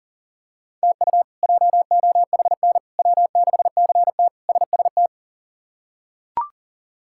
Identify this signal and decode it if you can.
Morse code, heard as a keyed tone, about 40 wpm, 700 Hz. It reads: TU JOHN W6CT SST
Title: K1USN SST QSO Practice - 40wpm